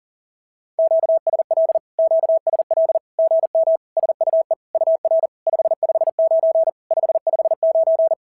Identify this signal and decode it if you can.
Morse code, heard as a keyed tone, about 42 wpm, 650 Hz. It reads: QSL QSL GM SUE UR 559 559
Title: SOTA QSO Practice 40wpm